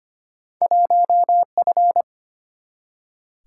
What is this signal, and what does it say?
Morse code, heard as a keyed tone, about 25 wpm, 700 Hz. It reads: 1F
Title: Single-Letter Single-Number 25wpm